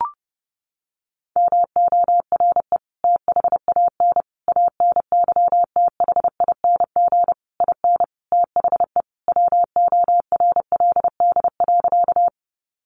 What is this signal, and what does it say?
Morse code, heard as a keyed tone, about 30 wpm, 700 Hz. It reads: MORE THAN ANYTHING IN THE WORLD.
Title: Sentences from Top 1000 Words 30wpm